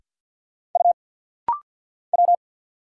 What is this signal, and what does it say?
Morse code, heard as a keyed tone, about 50 wpm, 700 Hz. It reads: U W
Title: Single Letter-Number Alt - Rapid-Fire 50wpm